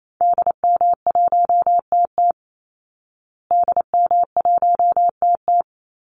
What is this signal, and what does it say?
Morse code, heard as a keyed tone, about 28 wpm, 700 Hz. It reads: DM1TT DM1TT
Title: Call Signs Repeated 3x - 28wpm